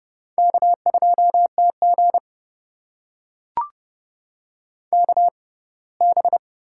Morse code, heard as a keyed tone, about 30 wpm, spaced as for 20 wpm, 700 Hz. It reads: K2TG KB
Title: USA Call Signs - 4 Characters - 6x Character Spacing - 30wpm